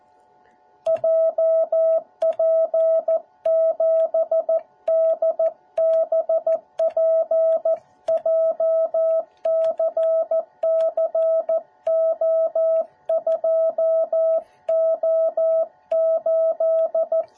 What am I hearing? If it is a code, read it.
Morse code, heard as a keyed tone, about 14 wpm, 650 Hz. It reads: JP7DBP JCCO2O8